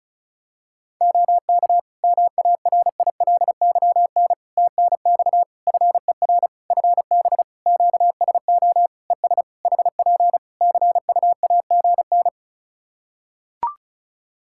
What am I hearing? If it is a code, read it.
Morse code, heard as a keyed tone, about 36 wpm, 700 Hz. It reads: OK MARILYN TNX FER FB QSO ES HP CUAGN
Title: Practice QSO - Exchange 4 - Ending - 35wpm